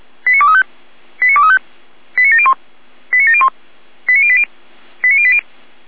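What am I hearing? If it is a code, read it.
Five-tone selective call (ZVEI), heard as quick runs of five tones, tones of about 70 ms. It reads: 89235 89235 89701 89701 89080 89080
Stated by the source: ZVEI five-tone selective call received by SDR.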